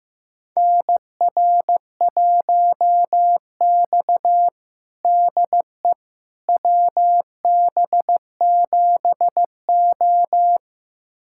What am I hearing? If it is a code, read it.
Morse code, heard as a keyed tone, about 15 wpm, 700 Hz. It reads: NR1X DE WB7O
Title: USA Callsign DE Callsign - 4 Characters - 15wpm